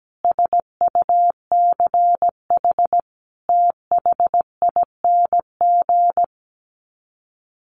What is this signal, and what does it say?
Morse code, heard as a keyed tone, about 17 wpm, 700 Hz. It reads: SUCH THING
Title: Sets of 5 Words Encore 17wpm